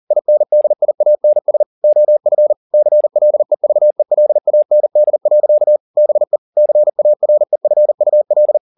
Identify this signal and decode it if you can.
Morse code, heard as a keyed tone, about 40 wpm, 600 Hz. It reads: INDIANS OF CLEVELAND. BE CAREFUL